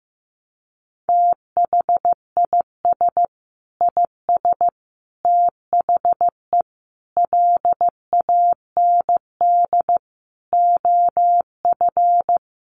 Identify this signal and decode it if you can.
Morse code, heard as a keyed tone, about 15 wpm, 700 Hz. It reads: THIS IS THE LAND OF
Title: Sentences from Top 1300 Words 15wpm